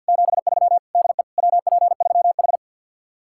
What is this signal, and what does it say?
Morse code, heard as a keyed tone, about 64 wpm, 700 Hz. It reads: 73 DE WP3H